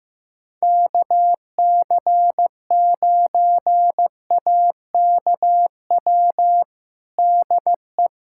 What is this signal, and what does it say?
Morse code, heard as a keyed tone, about 15 wpm, 700 Hz. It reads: KC9AKW DE
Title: USA Callsign DE Callsign 15wpm